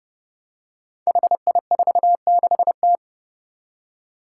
Morse code, heard as a keyed tone, about 30 wpm, 700 Hz. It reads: HI46T
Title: Five Challenging Letter-Number Combinations 30wpm